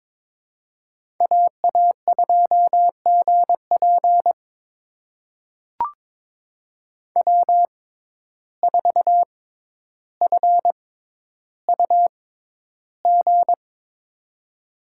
Morse code, heard as a keyed tone, about 22 wpm, spaced as for 11 wpm, 700 Hz. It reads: AA2GP W4FUG
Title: USA Call Signs - 5 Characters - 6x Character Spacing - 22wpm